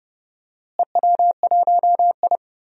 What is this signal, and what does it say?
Morse code, heard as a keyed tone, about 32 wpm, 700 Hz. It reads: EW1I
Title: DXCC-100 Call Signs - 4 Characters - Repeated 3x - 30wpm